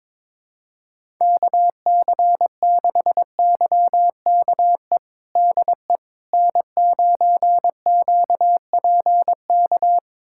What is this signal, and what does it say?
Morse code, heard as a keyed tone, about 22 wpm, 700 Hz. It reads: KC6YKE DE N9QPK
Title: USA Callsign DE Callsign 22wpm